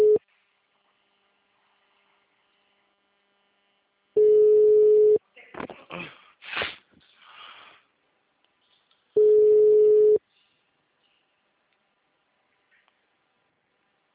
Ringback tone (Europe, 425 Hz).